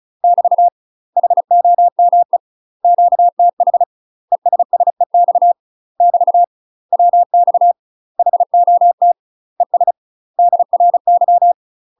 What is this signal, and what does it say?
Morse code, heard as a keyed tone, about 36 wpm, 700 Hz. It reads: <BT> HOME QTH ESSEX <BT> WX HOT ES DRY
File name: UK MASTER QSOs TRAINING / 35WPM / UK MASTER QSOs 0031